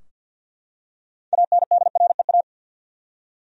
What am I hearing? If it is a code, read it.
Morse code, heard as a keyed tone, about 45 wpm, 700 Hz. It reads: ANDREA
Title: USA Common Names - 50wpm